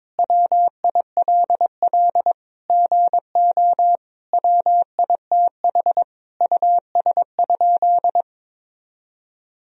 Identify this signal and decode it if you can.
Morse code, heard as a keyed tone, about 22 wpm, 700 Hz. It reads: WILL GO WITH US?
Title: Sentences from Top 1000 Words - Review Part 3 - 22wpm